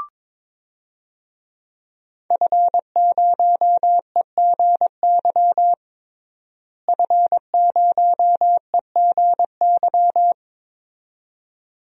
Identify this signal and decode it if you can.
Morse code, heard as a keyed tone, about 22 wpm, 700 Hz. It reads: F0EGY F0EGY
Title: DXCC-100 Call Signs - 5 Characters - Repeated 2x - 22wpm